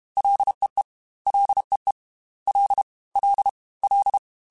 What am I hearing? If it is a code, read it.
Morse code, heard as a keyed tone, about 32 wpm, 800 Hz. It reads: LEE LEE L L L
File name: F and L / F and L mp3 / F and L Prefixes 3x2 Twice @30